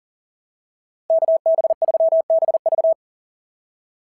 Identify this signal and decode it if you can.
Morse code, heard as a keyed tone, about 40 wpm, 660 Hz. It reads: KB3BV
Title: SOTA QSO Practice 40wpm